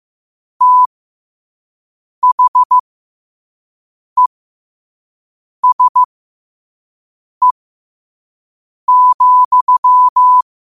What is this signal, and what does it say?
Morse code, heard as a keyed tone, about 15 wpm, spaced as for 5 wpm, 1000 Hz. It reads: THESE,
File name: 5wpm